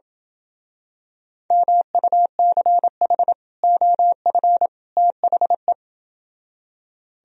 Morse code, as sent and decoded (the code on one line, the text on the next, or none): -- ..- -.-. .... / --- ..-. / - .... .
MUCH OF THE